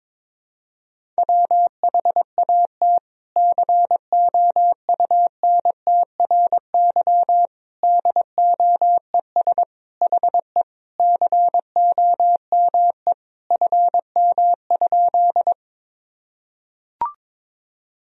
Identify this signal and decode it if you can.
Morse code, heard as a keyed tone, about 21 wpm, 700 Hz. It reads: WHAT COUNTRY DOES HE COME FM?
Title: Sentences from Top 300 Words and Abbreviations 22wpm